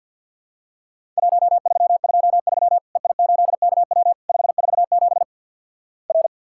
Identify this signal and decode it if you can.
Morse code, heard as a keyed tone, about 47 wpm, 700 Hz. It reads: 1323 EI8KW 547 R
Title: Worked All Europe DX Contest - QTC Exchange - 50wpm